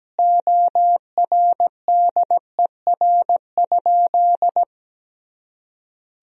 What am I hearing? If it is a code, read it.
Morse code, heard as a keyed tone, about 17 wpm, 700 Hz. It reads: ORDER?
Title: Sentences from Top 400 Words and Abbreviations 17wpm